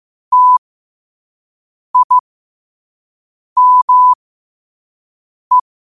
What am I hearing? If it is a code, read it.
Morse code, heard as a keyed tone, about 15 wpm, spaced as for 5 wpm, 1000 Hz. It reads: TIME
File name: qotdfiles / 5wpm